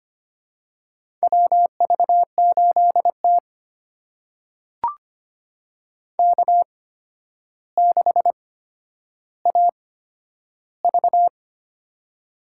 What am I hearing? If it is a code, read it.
Morse code, heard as a keyed tone, about 25 wpm, spaced as for 11 wpm, 700 Hz. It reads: WV8T K6AV
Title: USA Call Signs - 4 Characters - 8x Character Spacing - 25wpm